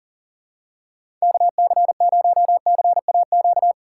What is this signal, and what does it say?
Morse code, heard as a keyed tone, about 40 wpm, 700 Hz. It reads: KC0CAQ
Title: USA Call Signs - 6 Characters - 3x Character Spacing - 40wpm